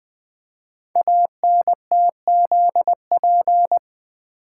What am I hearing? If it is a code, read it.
Morse code, heard as a keyed tone, about 20 wpm, 700 Hz. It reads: ANTZP
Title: Code Groups - 5 Letters - 20wpm